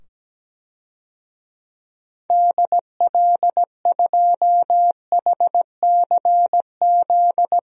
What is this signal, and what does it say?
Morse code, heard as a keyed tone, about 17 wpm, 700 Hz. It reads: DL2HCZ
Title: DXCC-100 Call Signs - 6 Characters - 17wpm